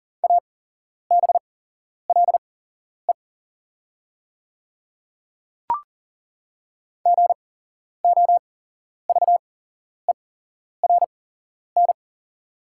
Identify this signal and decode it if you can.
Morse code, heard as a keyed tone, about 40 wpm, spaced as for 11 wpm, 690 Hz. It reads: ABLE GOVERN
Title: 8x Character Spacing - Top 500 Words - 40wpm